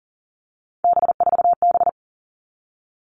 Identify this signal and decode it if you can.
Morse code, heard as a keyed tone, about 38 wpm, 700 Hz. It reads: B4B
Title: All Cut Numbers - 3-Digit 40wpm